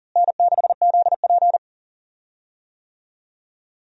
Morse code, heard as a keyed tone, about 40 wpm, 700 Hz. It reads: N6ZP
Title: USA Call Signs - 4 Characters - Repeated 3x - 40wpm